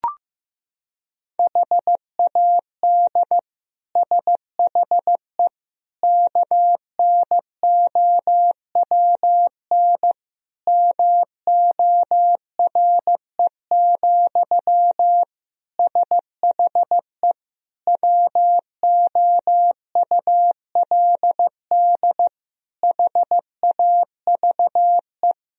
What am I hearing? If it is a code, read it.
Morse code, heard as a keyed tone, about 15 wpm, 700 Hz. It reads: HAD SHE KNOWN MORE, SHE WOULD HAVE
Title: Sentences from Top 1000 Words - Review Part 3 - 15wpm